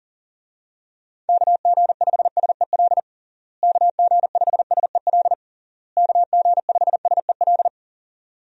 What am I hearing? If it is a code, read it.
Morse code, heard as a keyed tone, about 40 wpm, 700 Hz. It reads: KG5SEL KG5SEL KG5SEL
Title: USA Call Signs - 6 Characters - Repeated 3x - 40wpm